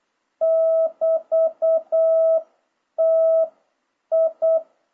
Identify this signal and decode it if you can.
Morse code, heard as a keyed tone, about 8 wpm, spaced as for 7 wpm, 650 Hz. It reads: <BT>TI
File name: jh8cbh / 2026cw